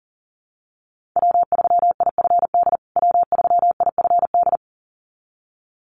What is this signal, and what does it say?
Morse code, heard as a keyed tone, about 40 wpm, 700 Hz. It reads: W3IFD W3IFD
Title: USA Call Signs Repeated 2x 40wpm